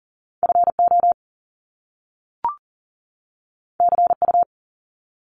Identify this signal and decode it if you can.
Morse code, heard as a keyed tone, about 40 wpm, 700 Hz. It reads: FO CU
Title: Top Two Letter Combinations in Words 40wpm